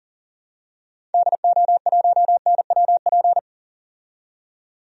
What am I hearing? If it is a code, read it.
Morse code, heard as a keyed tone, about 37 wpm, 700 Hz. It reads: DO1NWP
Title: DXCC-100 Call Signs - 6 Characters - 40wpm